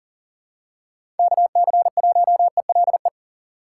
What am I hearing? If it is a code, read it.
Morse code, heard as a keyed tone, about 40 wpm, 700 Hz. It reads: KC1ELE
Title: USA Call Signs - 6 Characters - 4x Character Spacing - 40wpm